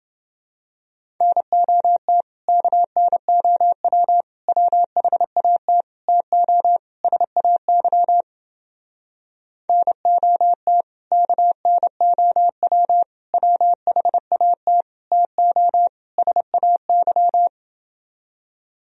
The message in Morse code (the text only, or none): NOT KNOW WHAT TO SAY NOT KNOW WHAT TO SAY